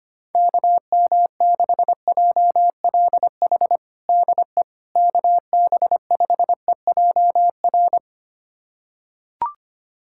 Morse code, heard as a keyed tone, about 25 wpm, 700 Hz. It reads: KM6JLH DE KB5EJR
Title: USA Callsign DE Callsign - 6 Characters - 25wpm